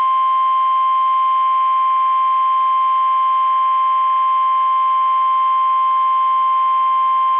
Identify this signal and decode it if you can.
Emergency Alert System (SAME) tones (the 1050 Hz attention tone). It is heard from 1.2 to 6.5 s.